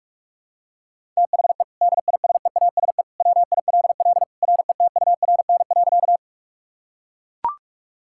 Morse code, heard as a keyed tone, about 42 wpm, 700 Hz. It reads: THE DISEASE WILL RETURN.
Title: Sentences from Top 1000 Words - Review Part 6 - 45wpm